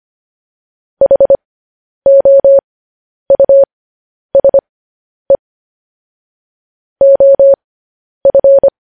Morse code, heard as a keyed tone, about 25 wpm, spaced as for 10 wpm, 550 Hz. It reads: HOUSE OF
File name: podcast / News headlines-1x-25x10